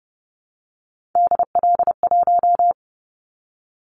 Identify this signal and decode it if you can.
Morse code, heard as a keyed tone, about 30 wpm, 700 Hz. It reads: DL1
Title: Callsign Prefixes for Top 50 DXCC - 30wpm